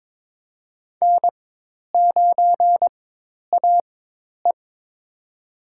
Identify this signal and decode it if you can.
Morse code, heard as a keyed tone, about 21 wpm, spaced as for 10 wpm, 700 Hz. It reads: N9AE
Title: USA Call Signs - 4 Characters - 4x Character Spacing - 22wpm